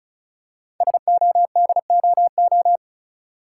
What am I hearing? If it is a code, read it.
Morse code, heard as a keyed tone, about 36 wpm, 700 Hz. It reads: SODOO
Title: Code Groups - 5 Letters - 35wpm